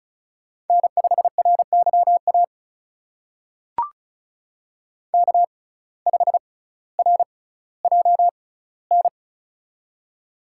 Morse code, heard as a keyed tone, about 36 wpm, spaced as for 18 wpm, 700 Hz. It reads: N5RYA K5RJN
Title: USA Call Signs - 5 Characters - 6x Character Spacing - 35wpm